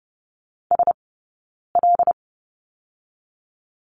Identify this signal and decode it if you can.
Morse code, heard as a keyed tone, about 30 wpm, 700 Hz. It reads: S L
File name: Lesson 19d - Letter w - Two Characters 30wpm